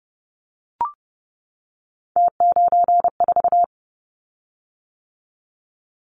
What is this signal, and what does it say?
Morse code, heard as a keyed tone, about 30 wpm, 700 Hz. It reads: T94
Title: Callsign Prefixes for Top 100 DXCC - 30wpm